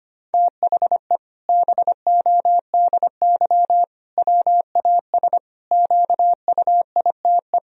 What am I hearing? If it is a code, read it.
Morse code, heard as a keyed tone, about 25 wpm, 700 Hz. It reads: THE BODY WAS QUITE
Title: Sentences from Top 1000 Words - Review Part 2 - 25wpm